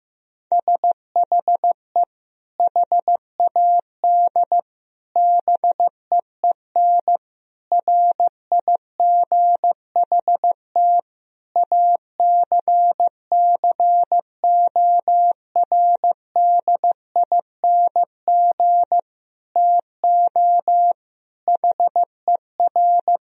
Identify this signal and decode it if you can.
Morse code, heard as a keyed tone, about 15 wpm, 700 Hz. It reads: SHE HAD BEEN RIGHT ACCORDING TO HER
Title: Sentences from Top 1000 Words - Review Part 4 - 15wpm